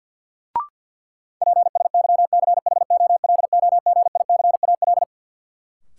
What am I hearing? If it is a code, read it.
Morse code, heard as a keyed tone, about 52 wpm, 700 Hz. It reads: PSYCHOLOGICAL
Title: Sentences from Top 1400 Words - New Word Drill - 50wpm